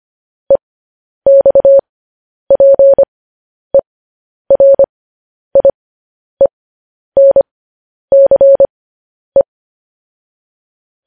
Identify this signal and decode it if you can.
Morse code, heard as a keyed tone, about 25 wpm, spaced as for 10 wpm, 550 Hz. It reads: EXPERIENCE
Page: News Headlines in Morse Code at 10 WPM